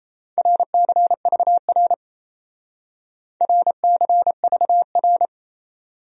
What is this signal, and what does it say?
Morse code, heard as a keyed tone, about 35 wpm, 700 Hz. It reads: RCVR RCVR
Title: Speed-Racing Abbreviations - 28wpm